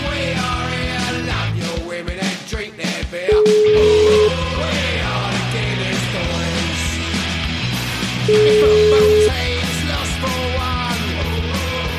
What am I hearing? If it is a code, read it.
Ringback tone (Europe, 425 Hz), with noise.